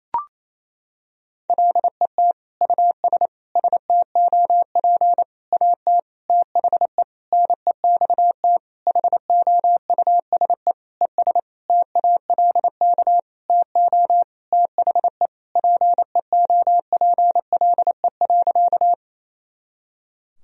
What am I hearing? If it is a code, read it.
Morse code, heard as a keyed tone, about 28 wpm, 700 Hz. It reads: LET US STOP AT THE NEXT HOUSE ES TALK TO THE PEOPLE.
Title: Sentences from Top 400 Words and Abbreviations 28wpm